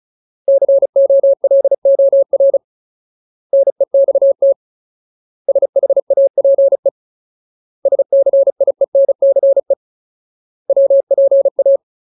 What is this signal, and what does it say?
Morse code, heard as a keyed tone, about 35 wpm, 550 Hz. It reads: COLOR NEXT SHAPE SCIENCE WPA
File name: Ultimate Word List / 35WPM Ultimate